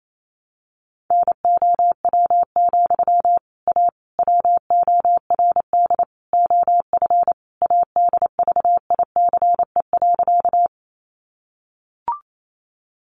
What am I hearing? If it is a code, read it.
Morse code, heard as a keyed tone, about 28 wpm, 700 Hz. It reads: NOW, A WORD OF ADVICE.
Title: Sentences from Top 1100 Words 28wpm